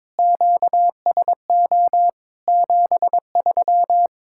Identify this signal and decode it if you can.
Morse code, heard as a keyed tone, about 22 wpm, 700 Hz. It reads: QSO 73
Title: K1USN SST QSO Practice 22wpm